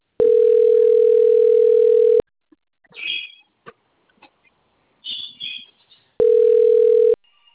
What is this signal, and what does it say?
Ringback tone (North America), 440 + 480 Hz.